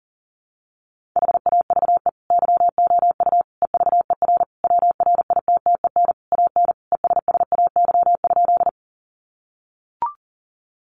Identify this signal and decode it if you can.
Morse code, heard as a keyed tone, about 40 wpm, 700 Hz. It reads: HAVE YOU EVER WRITTEN AN ESSAY?
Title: Sentences from Top 1500 Words 40wpm